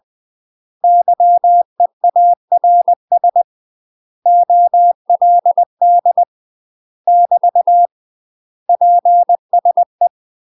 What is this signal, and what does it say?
Morse code, heard as a keyed tone, about 20 wpm, 700 Hz. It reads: YEARS OLD <BT> PSE
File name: UK MASTER QSOs TRAINING / 20WPM / UK MASTER QSOs 0037